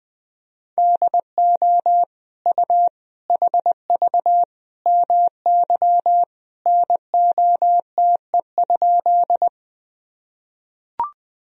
Morse code, heard as a keyed tone, about 21 wpm, 700 Hz. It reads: DO U HV MY NOTE?
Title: Sentences from Top 500 Words and Abbreviations 20wpm